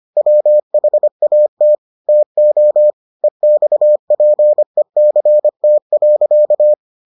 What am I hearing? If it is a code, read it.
Morse code, heard as a keyed tone, about 25 wpm, 600 Hz. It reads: WHAT TO EXPECT.